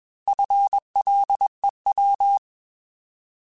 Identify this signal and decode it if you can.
Morse code, heard as a keyed tone, about 21 wpm, 800 Hz. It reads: FLEW
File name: Alphabetical / Alphabetical mp3s / Fs @20